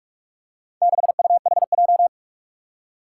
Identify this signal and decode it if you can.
Morse code, heard as a keyed tone, about 44 wpm, 700 Hz. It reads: 6UHJ
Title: Four Challenging Letter-Number Combinations 45wpm